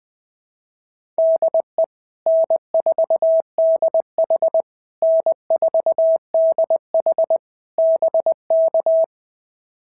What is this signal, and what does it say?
Morse code, heard as a keyed tone, about 19 wpm, 650 Hz. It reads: DE N4DH N4DH BK
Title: POTA QSO Practice - 20wpm